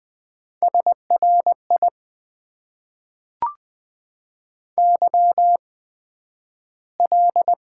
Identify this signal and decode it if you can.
Morse code, heard as a keyed tone, about 20 wpm, spaced as for 11 wpm, 700 Hz. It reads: SRI YL
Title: Top 90 QSO Elements - 8x Character Spacing - 20wpm